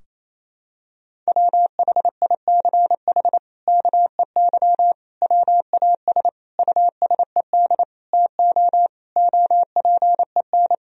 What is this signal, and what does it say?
Morse code, heard as a keyed tone, about 28 wpm, 700 Hz. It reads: WHICH KEY WAS USED TO OPEN
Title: Sentences from Top 1000 Words - Review Part 4 - 28wpm